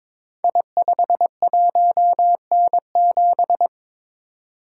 Morse code, heard as a keyed tone, about 22 wpm, 700 Hz. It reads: I51N7
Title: Code Groups - 5 Letter:Numbers - 22wpm